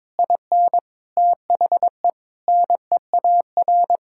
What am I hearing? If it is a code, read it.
Morse code, heard as a keyed tone, about 22 wpm, 700 Hz. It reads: IN THE NEAR
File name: Sentences with 7-Word Phrases - 22wpm